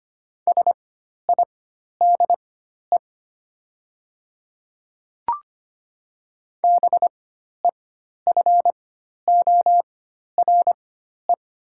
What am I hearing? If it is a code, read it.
Morse code, heard as a keyed tone, about 25 wpm, spaced as for 12 wpm, 700 Hz. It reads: SIDE BEFORE
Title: 4x Character Spacing - Top 300 Words - 25wpm